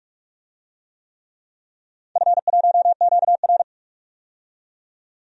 Morse code, heard as a keyed tone, about 46 wpm, 700 Hz. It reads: F1QR
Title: DXCC-100 Call Signs - 4 Characters - 45wpm